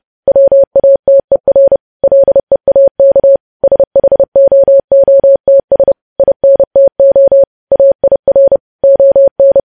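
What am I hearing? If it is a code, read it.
Morse code, heard as a keyed tone, about 30 wpm, 550 Hz. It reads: WATER LEAK SHOOTS INTO AIR ON
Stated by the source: Morse code news headlines practice at 30 WPM.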